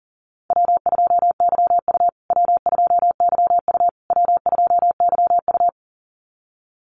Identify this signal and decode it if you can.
Morse code, heard as a keyed tone, about 40 wpm, 700 Hz. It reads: W2YU W2YU W2YU
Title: USA Call Signs Repeated 3x 40wpm